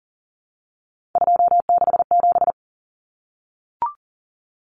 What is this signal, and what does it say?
Morse code, heard as a keyed tone, about 40 wpm, 700 Hz. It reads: 267
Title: Common Cut Numbers - 3-Digit 40wpm